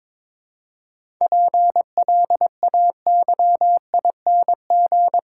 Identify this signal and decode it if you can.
Morse code, heard as a keyed tone, about 22 wpm, 700 Hz. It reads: PLAYING
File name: Difficult Sentences with Hints - 22wpm